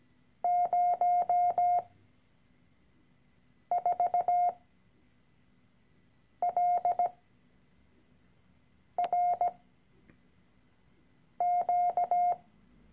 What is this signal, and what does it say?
Morse code, heard as a keyed tone, about 17 wpm, spaced as for 4 wpm, 710 Hz. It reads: 04LRQ
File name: CW-Online-Kurs / Anfänger / Lektion 013